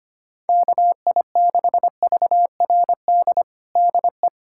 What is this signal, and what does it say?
Morse code, heard as a keyed tone, about 25 wpm, 700 Hz. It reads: KI6VRD DE
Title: USA Callsign DE Callsign 25wpm